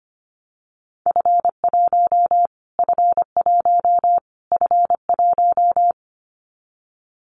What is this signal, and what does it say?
Morse code, heard as a keyed tone, about 25 wpm, 700 Hz. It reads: F1 F1 F1
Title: Callsign Prefixes for Top 25 DXCC - Repeated 3x - 25wpm